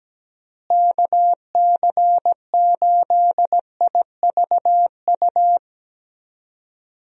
Morse code, heard as a keyed tone, about 17 wpm, 700 Hz. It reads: KC8IVU
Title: USA Call Signs - 6 Characters - Repeated 3x - 17wpm